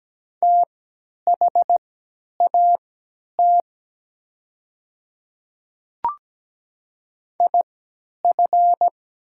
Morse code, heard as a keyed tone, about 17 wpm, spaced as for 10 wpm, 700 Hz. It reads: THAT IF
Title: 3x Character Spacing - Top 100 Words - 17wpm